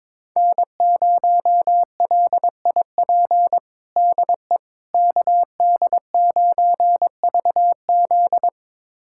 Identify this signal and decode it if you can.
Morse code, heard as a keyed tone, about 22 wpm, 700 Hz. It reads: N0LIP DE KD9VZ
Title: USA Callsign DE Callsign - 5 Characters - 22wpm